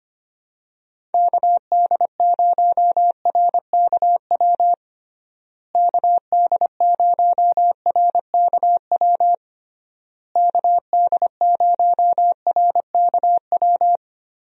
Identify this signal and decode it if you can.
Morse code, heard as a keyed tone, about 25 wpm, 700 Hz. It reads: KD0RKW KD0RKW KD0RKW
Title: USA Call Signs - 6 Characters - Repeated 3x - 25wpm